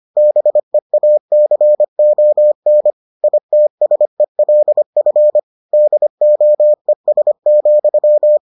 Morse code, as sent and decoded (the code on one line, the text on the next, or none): -... .- -.-. --- -. / .. - ... .-.. ..-. / -.. --- . ... --..--
BEACON ITSELF DOES,